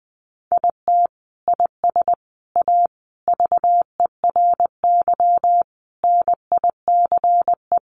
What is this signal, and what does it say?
Morse code, heard as a keyed tone, about 20 wpm, 700 Hz. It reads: IT IS A VERY NICE
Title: Sentences from Top 1100 Words 20wpm